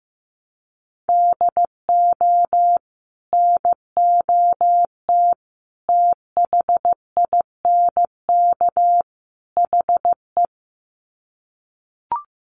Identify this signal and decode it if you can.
Morse code, heard as a keyed tone, about 15 wpm, 700 Hz. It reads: DO NOT THINK HE
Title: Sets of 4 Words 15wpm